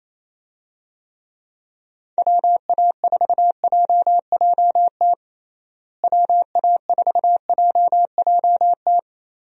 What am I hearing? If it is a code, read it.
Morse code, heard as a keyed tone, about 28 wpm, 700 Hz. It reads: WA4JJT WA4JJT
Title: USA Call Signs - 6 Characters - Repeated 2x - 28wpm